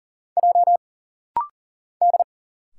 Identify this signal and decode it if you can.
Morse code, heard as a keyed tone, about 40 wpm, 700 Hz. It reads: J D
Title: Single Letter-Number Alt - Rapid-Fire 40wpm